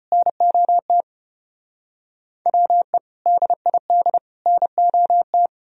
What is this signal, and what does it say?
Morse code, heard as a keyed tone, about 33 wpm, 700 Hz. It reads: NOT WE DID NOT